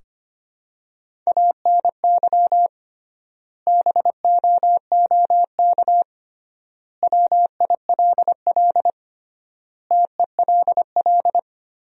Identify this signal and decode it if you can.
Morse code, heard as a keyed tone, about 25 wpm, 700 Hz. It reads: ANY BOOK WILL TELL
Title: 3x Word-Spacing - Sentences from Top 300 Words - 25wpm